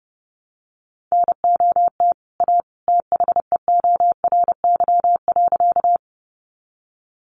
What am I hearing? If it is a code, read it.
Morse code, heard as a keyed tone, about 30 wpm, 700 Hz. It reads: NOT A THEORY.
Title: Sentences from Top 600 Words 30wpm